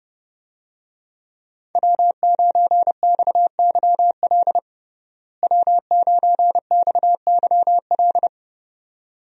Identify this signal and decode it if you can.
Morse code, heard as a keyed tone, about 30 wpm, 700 Hz. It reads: W9XYL W9XYL
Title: USA Call Signs - 5 Characters - Repeated 2x - 30wpm